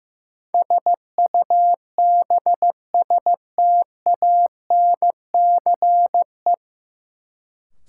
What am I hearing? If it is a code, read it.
Morse code, heard as a keyed tone, about 15 wpm, 700 Hz. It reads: SUBSTANCE
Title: Sentences from Top 1500 Words - New Word Drill - 15wpm